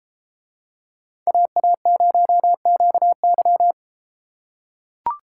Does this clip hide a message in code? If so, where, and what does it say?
Yes: Morse code from 1.3 to 3.7 s, heard as a keyed tone, about 32 wpm, 700 Hz. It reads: AA0QY